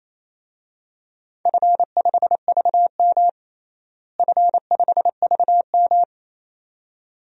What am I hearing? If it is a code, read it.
Morse code, heard as a keyed tone, about 28 wpm, 700 Hz. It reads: F5VM F5VM
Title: DXCC-100 Call Signs - 4 Characters - Repeated 2x - 28wpm